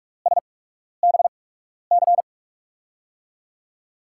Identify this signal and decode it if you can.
Morse code, heard as a keyed tone, about 44 wpm, spaced as for 12 wpm, 700 Hz. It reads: SBC
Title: USA Call Signs - 5 Characters - 8x Character Spacing - 45wpm